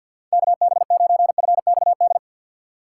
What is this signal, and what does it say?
Morse code, heard as a keyed tone, about 47 wpm, 700 Hz. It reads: KB9FXD
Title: USA Call Signs - 3x Character Spacing - 50wpm